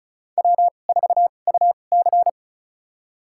Morse code, heard as a keyed tone, about 34 wpm, spaced as for 25 wpm, 700 Hz. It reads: W4UC
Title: USA Call Signs - 4 Characters - 2x Character Spacing - 35wpm